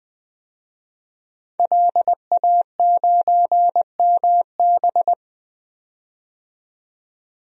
Morse code, heard as a keyed tone, about 20 wpm, 700 Hz. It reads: LA9MB
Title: DXCC-100 Call Signs - 5 Characters - 20wpm